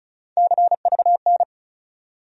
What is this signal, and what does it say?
Morse code, heard as a keyed tone, about 34 wpm, 700 Hz. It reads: CVN
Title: USA Call Sign Postfixes - 3 Letters 35wpm